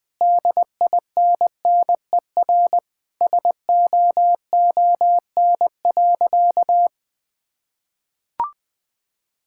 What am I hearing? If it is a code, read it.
Morse code, heard as a keyed tone, about 20 wpm, 700 Hz. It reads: DINNER SOON.
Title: Sentences with 3-Word Phrases - 20wpm